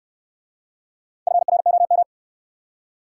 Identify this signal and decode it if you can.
Morse code, heard as a keyed tone, about 66 wpm, 700 Hz. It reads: FROM